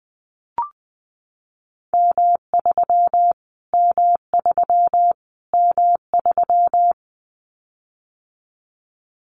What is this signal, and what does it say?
Morse code, heard as a keyed tone, about 20 wpm, 700 Hz. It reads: M3 M3 M3
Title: Callsign Prefixes for Top 25 DXCC - Repeated 3x - 20wpm